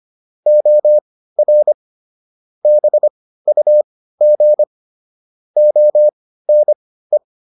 Morse code, heard as a keyed tone, about 25 wpm, spaced as for 15 wpm, 600 Hz. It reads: OR BUG ONE